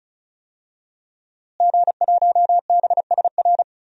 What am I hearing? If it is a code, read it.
Morse code, heard as a keyed tone, about 34 wpm, 700 Hz. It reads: G1BSR
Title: DXCC-100 Call Signs - 5 Characters - Repeated 3x - 35wpm